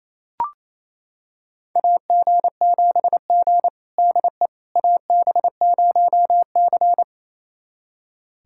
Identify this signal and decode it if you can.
Morse code, heard as a keyed tone, about 28 wpm, 700 Hz. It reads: AG7G DE AB0C
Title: USA Callsign DE Callsign - 4 Characters - 28wpm